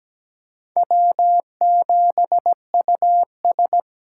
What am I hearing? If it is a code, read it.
Morse code, heard as a keyed tone, about 17 wpm, 700 Hz. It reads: W7US
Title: USA Call Signs - 4 Characters - 6x Character Spacing - 17wpm